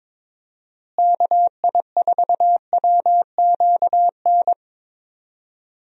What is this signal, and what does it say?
Morse code, heard as a keyed tone, about 22 wpm, 700 Hz. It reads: KI4WQN
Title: USA Call Signs - 6 Characters - Repeated 2x - 22wpm